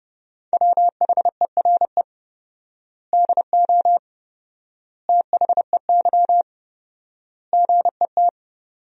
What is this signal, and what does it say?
Morse code, heard as a keyed tone, about 30 wpm, 700 Hz. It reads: WHERE DO THEY GET
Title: 4x Word-Spacing - Sentences from Top 200 Words - 30wpm